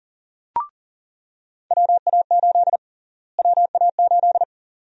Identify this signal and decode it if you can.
Morse code, heard as a keyed tone, about 40 wpm, 700 Hz. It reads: WA8 WA8
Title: USA Call Sign Prefixes - Repeated 2x - 40wpm